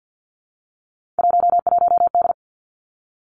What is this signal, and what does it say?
Morse code, heard as a keyed tone, about 50 wpm, 700 Hz. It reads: 11D
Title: Three Challenging Letter-Number Combinations 50wpm